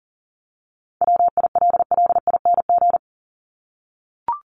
Morse code, heard as a keyed tone, about 40 wpm, 700 Hz. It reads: WILLING
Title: Top 751-1000 Words 40wpm